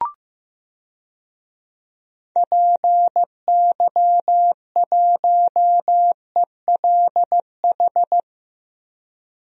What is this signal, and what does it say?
Morse code, heard as a keyed tone, about 15 wpm, 700 Hz. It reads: PY1ELH
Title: DXCC-100 Call Signs - 6 Characters - Repeated 3x - 15wpm